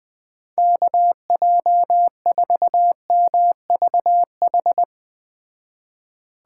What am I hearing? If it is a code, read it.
Morse code, heard as a keyed tone, about 21 wpm, 700 Hz. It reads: KJ4MVH
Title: Speed Racing - USA Call Signs 17wpm